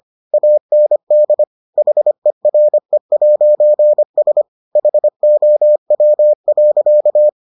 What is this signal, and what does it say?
Morse code, heard as a keyed tone, about 25 wpm, 600 Hz. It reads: AND HERE'S HOW.